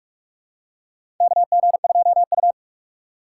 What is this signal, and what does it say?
Morse code, heard as a keyed tone, about 46 wpm, 700 Hz. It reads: KG2U
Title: USA Call Signs - 4 Characters - 3x Character Spacing - 45wpm